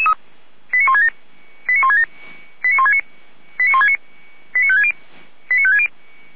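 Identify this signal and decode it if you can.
Five-tone selective call (ZVEI), heard as quick runs of five tones, tones of about 70 ms. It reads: 89167 89167 89169 89169 89560 89560
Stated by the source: ZVEI five-tone selective call 89220 received on SDR, from feuerwehr-orlamuende fire brigade audio server.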